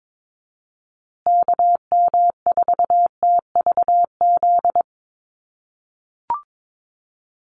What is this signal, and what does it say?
Morse code, heard as a keyed tone, about 22 wpm, 700 Hz. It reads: KM4TVZ
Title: USA Call Signs Repeated 3x 22wpm